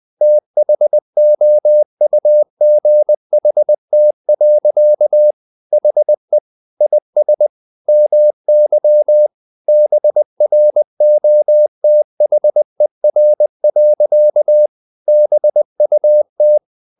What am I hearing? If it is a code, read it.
Morse code, heard as a keyed tone, about 20 wpm, 600 Hz. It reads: THOUGHT. HE IS MY BROTHER. BUT
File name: the old man and the sea / the old man and the sea part 1 20wpm